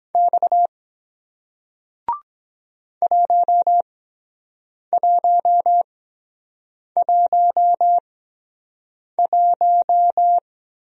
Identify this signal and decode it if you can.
Morse code, heard as a keyed tone, about 22 wpm, spaced as for 7 wpm, 700 Hz. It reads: X 1111